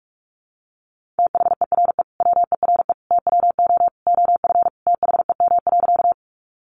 Morse code, heard as a keyed tone, about 44 wpm, 700 Hz. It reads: THERE WERE TWO OF THEM.
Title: Sentences from Top 100 Words 45wpm